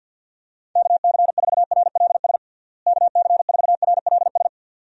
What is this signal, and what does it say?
Morse code, heard as a keyed tone, about 51 wpm, 700 Hz. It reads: KC4RLS KC4RLS
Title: USA Call Signs - 6 Characters - Repeated 2x - 50wpm